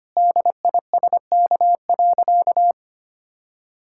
Morse code, heard as a keyed tone, about 25 wpm, 700 Hz. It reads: DISK.
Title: Sentences from Top 1000 Words - Review Part 6 - 25wpm